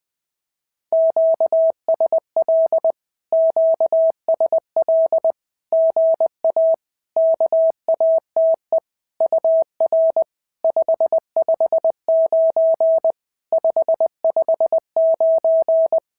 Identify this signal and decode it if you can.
Morse code, heard as a keyed tone, about 20 wpm, 650 Hz. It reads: QSL QSL GA KATE UR 559 559
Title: SOTA QSO Practice 20wpm